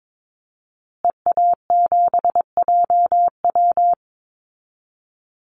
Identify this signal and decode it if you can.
Morse code, heard as a keyed tone, about 22 wpm, 700 Hz. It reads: EA7JW
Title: Call Signs Repeated 2x - 22wpm